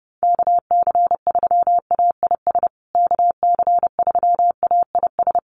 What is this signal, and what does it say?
Morse code, heard as a keyed tone, about 30 wpm, 700 Hz. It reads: KC3AIS KC3AIS
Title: USA Call Signs Repeated 3x 30wpm